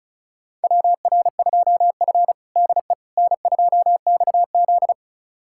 Morse code, heard as a keyed tone, about 36 wpm, 700 Hz. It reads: WR2F DE N2XZ
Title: USA Callsign DE Callsign - 4 Characters - 35wpm